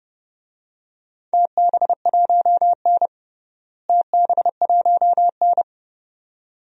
Morse code, heard as a keyed tone, about 30 wpm, 700 Hz. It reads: TB1N TB1N
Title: DXCC-100 Call Signs - 4 Characters - Repeated 2x - 30wpm